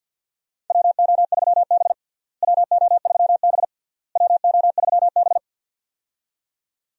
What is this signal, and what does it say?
Morse code, heard as a keyed tone, about 47 wpm, 700 Hz. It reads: WO3B WO3B WO3B
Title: USA Call Signs - 4 Characters - Repeated 3x - 50wpm